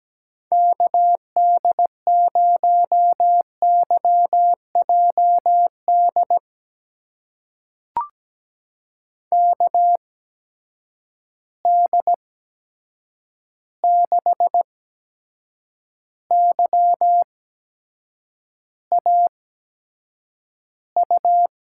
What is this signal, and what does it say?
Morse code, heard as a keyed tone, about 17 wpm, spaced as for 7 wpm, 700 Hz. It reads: KD0YJD KD6YAU
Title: USA Call Signs - 6 Characters - 8x Character Spacing - 17wpm